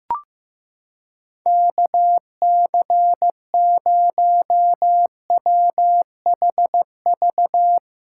Morse code, heard as a keyed tone, about 16 wpm, 700 Hz. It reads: KC0WHV
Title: USA Callsign DE Callsign 15wpm